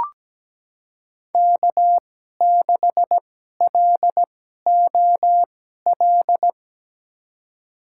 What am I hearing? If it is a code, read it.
Morse code, heard as a keyed tone, about 17 wpm, spaced as for 12 wpm, 700 Hz. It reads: K6LOL
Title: USA Call Signs - 2x Character Spacing - 17wpm